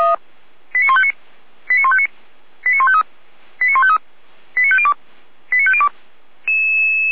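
Five-tone selective call (ZVEI), heard as quick runs of five tones, tones of about 70 ms. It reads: 89159 89159 89153 89153 89502 89502
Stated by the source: ZVEI five-tone selective call 89170, received on SDR.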